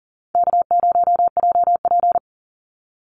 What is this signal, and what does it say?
Morse code, heard as a keyed tone, about 40 wpm, 700 Hz. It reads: K0JP
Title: Call Signs - 40wpm